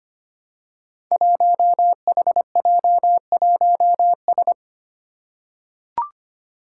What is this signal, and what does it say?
Morse code, heard as a keyed tone, about 25 wpm, 700 Hz. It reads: 1HJ1S
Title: Five Challenging Letter-Number Combinations 25wpm